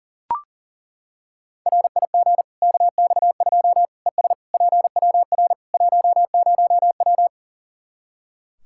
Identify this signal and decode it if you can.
Morse code, heard as a keyed tone, about 40 wpm, 700 Hz. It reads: RIG KX2 ES PWR 10W
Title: K1USN SST QSO Practice 40wpm